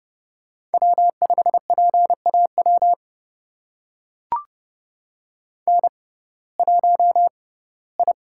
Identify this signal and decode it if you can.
Morse code, heard as a keyed tone, about 30 wpm, spaced as for 18 wpm, 700 Hz. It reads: W5PAW N1I
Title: USA Call Signs - 5 Characters - 6x Character Spacing - 30wpm